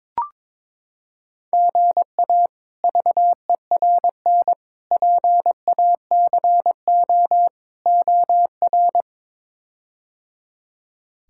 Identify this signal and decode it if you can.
Morse code, heard as a keyed tone, about 21 wpm, 700 Hz. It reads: GA VERN PACO OR
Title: K1USN SST QSO Practice - 22wpm